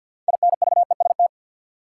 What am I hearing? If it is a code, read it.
Morse code, heard as a keyed tone, about 53 wpm, 700 Hz. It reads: INVEST
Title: Sentences from Top 1100 Words - New Word Drill - 50wpm